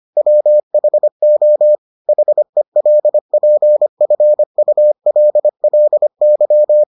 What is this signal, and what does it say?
Morse code, heard as a keyed tone, about 25 wpm, 600 Hz. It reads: WHO HELPFULLY